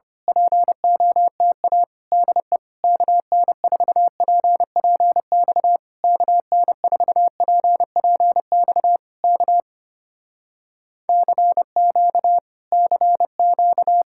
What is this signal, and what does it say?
Morse code, heard as a keyed tone, about 24 wpm, 700 Hz. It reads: POTA DE KN4PPX KN4PPX K CQ CQ
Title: Speed-Racing Common QSO Phrases - 25wpm